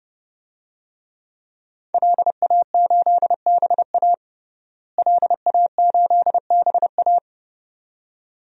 Morse code, heard as a keyed tone, about 32 wpm, 700 Hz. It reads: LA8BA LA8BA
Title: DXCC-100 Call Signs - 5 Characters - Repeated 2x - 30wpm